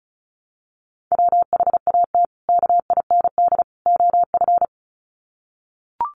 Morse code, heard as a keyed tone, about 34 wpm, 700 Hz. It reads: WHAT KIND OF
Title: Sets of 3 Words 35wpm